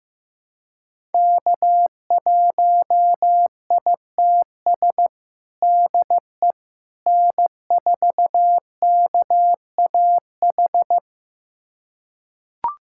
Morse code, heard as a keyed tone, about 15 wpm, 700 Hz. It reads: K1ITS DE N4KAH
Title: USA Callsign DE Callsign - 5 Characters - 15wpm